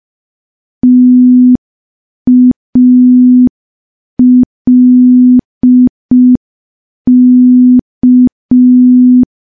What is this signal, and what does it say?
Morse code, heard as a keyed tone, about 5 wpm, 250 Hz. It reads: TALK
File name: morse / morse-training-05WPM